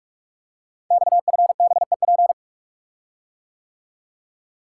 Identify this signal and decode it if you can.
Morse code, heard as a keyed tone, about 44 wpm, 700 Hz. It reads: XFBEP
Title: Code Groups - 5 Letter:Numbers - 45wpm